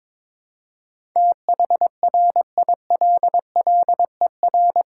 Morse code, heard as a keyed tone, about 22 wpm, 700 Hz. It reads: THRILLER
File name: Difficult Words with Hints - 22wpm